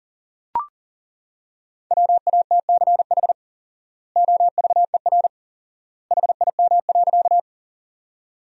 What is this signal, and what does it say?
Morse code, heard as a keyed tone, about 40 wpm, 700 Hz. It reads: WATCH OVER HIM.
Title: 4x Word-Spacing - Sentences from Top 300 Words - 40wpm